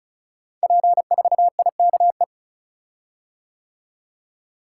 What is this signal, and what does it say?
Morse code, heard as a keyed tone, about 36 wpm, 690 Hz. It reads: P4IKE
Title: Code Groups - 5 Letter:Numbers - 35wpm